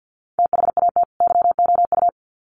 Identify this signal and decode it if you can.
Morse code, heard as a keyed tone, about 50 wpm, 700 Hz. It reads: THAT YOU
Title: Sets of 2 Words 50wpm